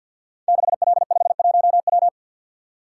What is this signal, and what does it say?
Morse code, heard as a keyed tone, about 52 wpm, 700 Hz. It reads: 6L51W
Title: Code Groups - 5 Letter:Numbers - 50wpm